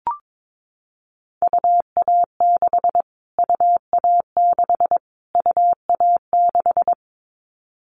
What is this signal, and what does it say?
Morse code, heard as a keyed tone, about 22 wpm, 700 Hz. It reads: UA6 UA6 UA6
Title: Callsign Prefixes for Top 25 DXCC - Repeated 3x - 22wpm